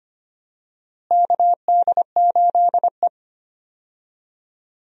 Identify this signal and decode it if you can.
Morse code, heard as a keyed tone, about 25 wpm, 700 Hz. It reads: KD8E
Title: USA Call Signs - 4 Characters - Repeated 2x - 25wpm